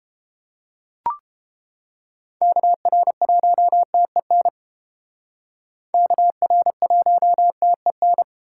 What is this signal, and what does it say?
Morse code, heard as a keyed tone, about 30 wpm, 700 Hz. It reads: KR1TEN KR1TEN